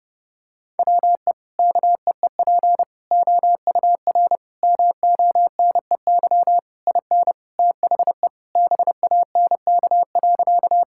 Morse code, heard as a keyed tone, about 30 wpm, 700 Hz. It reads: WE KEEP OUR MONEY IN THE BANK.
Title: Sentences from Top 1000 Words - Review Part 6 - 30wpm